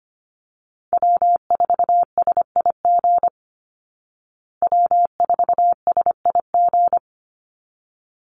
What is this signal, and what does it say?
Morse code, heard as a keyed tone, about 25 wpm, 700 Hz. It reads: W4SIG W4SIG
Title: Call Signs Repeated 3x - 25wpm